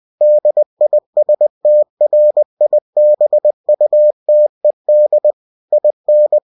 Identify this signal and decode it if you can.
Morse code, heard as a keyed tone, about 20 wpm, 600 Hz. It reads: DISTRIBUTED IN